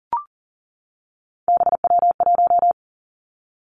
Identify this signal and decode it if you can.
Morse code, heard as a keyed tone, about 38 wpm, 700 Hz. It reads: BW1
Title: Three Challenging Letter-Number Combinations 40wpm